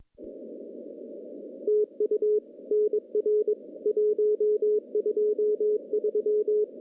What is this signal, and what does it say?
Morse code, heard as a keyed tone, about 23 wpm, 420 Hz. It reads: TU NR 123
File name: contests / CW-SS-k5tr- -1400